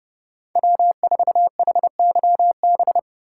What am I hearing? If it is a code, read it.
Morse code, heard as a keyed tone, about 34 wpm, 700 Hz. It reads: W4HYB
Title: Speed Racing - USA 1x3 Call Signs 30wpm